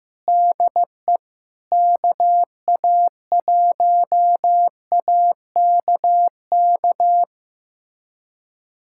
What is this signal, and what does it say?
Morse code, heard as a keyed tone, about 15 wpm, 700 Hz. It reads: DE KA1AKK
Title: USA Callsign DE Callsign - 6 Characters - 15wpm